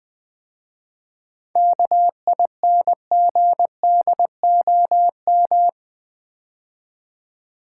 Morse code, heard as a keyed tone, about 21 wpm, 700 Hz. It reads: KINGDOM
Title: 7 Letter Words - First 3 Letters Given - 20wpm